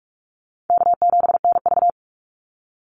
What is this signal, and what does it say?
Morse code, heard as a keyed tone, about 44 wpm, 700 Hz. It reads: K7NV
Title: Easy Call Signs Repeated 3x - 45wpm